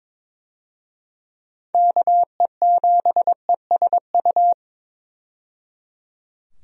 Morse code, heard as a keyed tone, about 22 wpm, 700 Hz. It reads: KE7ESU
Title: USA Call Signs - 6 Characters - 22wpm